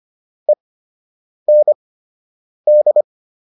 Morse code, heard as a keyed tone, about 25 wpm, spaced as for 8 wpm, 600 Hz. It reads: END